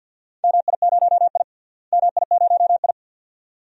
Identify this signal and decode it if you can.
Morse code, heard as a keyed tone, about 53 wpm, 700 Hz. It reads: MI0I MI0I
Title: DXCC-100 Call Signs - 4 Characters - Repeated 2x - 50wpm